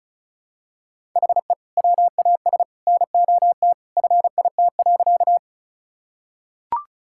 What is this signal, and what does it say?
Morse code, heard as a keyed tone, about 34 wpm, 690 Hz. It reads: HE WAS NOT FIT.
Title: Sentences from Top 1000 Words - Review Part 5 - 35wpm